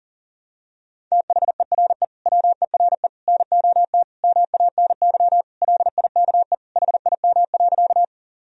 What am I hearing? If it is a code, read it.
Morse code, heard as a keyed tone, about 37 wpm, 700 Hz. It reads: THERE WERE NOT MANY LIKE HIM.
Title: Sentences from Top 1000 Words - Review Part 1 - 40wpm